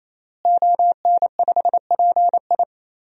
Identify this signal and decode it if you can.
Morse code, heard as a keyed tone, about 28 wpm, 700 Hz. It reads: ON5PI
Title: DXCC-100 Call Signs - 5 Characters - Repeated 3x - 28wpm